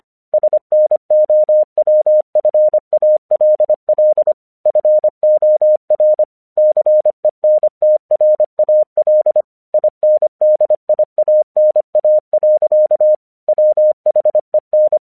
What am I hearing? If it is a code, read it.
Morse code, heard as a keyed tone, about 25 wpm, 610 Hz. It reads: SNOWFALL FOR CENTRAL INDIANA. WHEN
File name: 1 English 25wpm / AQC18